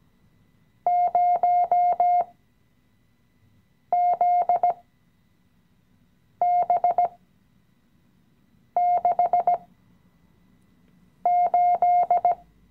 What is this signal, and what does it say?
Morse code, heard as a keyed tone, about 17 wpm, spaced as for 5 wpm, 710 Hz. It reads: 0ZB68
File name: CW-Online-Kurs / Anfänger / Lektion 017